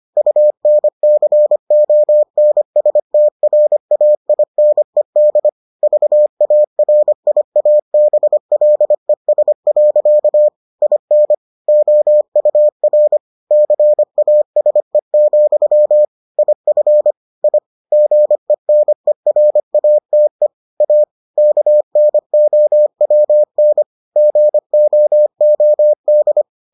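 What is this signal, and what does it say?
Morse code, heard as a keyed tone, about 25 wpm, 600 Hz. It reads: UNCONSTRAINED VARIABLES. IN OUR CASE, IF I GENERATE A KNOWN GOOD
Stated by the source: Morse edition of the Foundations of Amateur Radio episodes.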